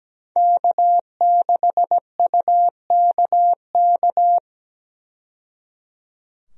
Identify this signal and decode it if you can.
Morse code, heard as a keyed tone, about 17 wpm, 700 Hz. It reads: K6UKK
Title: USA Call Signs - 5 Characters - 17wpm